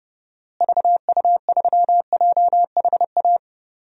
Morse code, heard as a keyed tone, about 29 wpm, 700 Hz. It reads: VU3JHA
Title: DXCC-100 Call Signs - 6 Characters - Repeated 2x - 30wpm